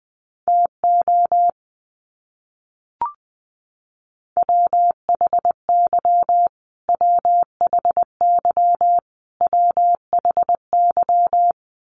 Morse code, heard as a keyed tone, about 20 wpm, 700 Hz. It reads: TO WHY WHY WHY
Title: Top 500 Words Repeated 20wpm